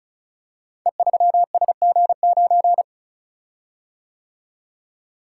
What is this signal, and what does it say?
Morse code, heard as a keyed tone, about 33 wpm, 700 Hz. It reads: E3SG9
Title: Code Groups - 5 Letter:Numbers - 35wpm